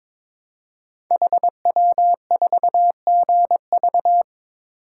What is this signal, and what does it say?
Morse code, heard as a keyed tone, about 22 wpm, 700 Hz. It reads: HW4GV
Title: Code Groups - 5 Letter:Numbers - 22wpm